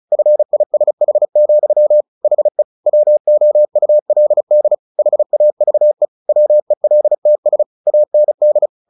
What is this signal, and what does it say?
Morse code, heard as a keyed tone, about 34 wpm, 600 Hz. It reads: FISH, HE WOULD HAVE WELTS AND
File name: the old man and the sea / the old man and the sea 1 35wpm